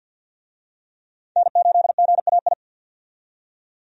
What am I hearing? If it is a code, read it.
Morse code, heard as a keyed tone, about 51 wpm, 700 Hz. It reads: N8GAI
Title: USA Call Signs - 5 Characters - 50wpm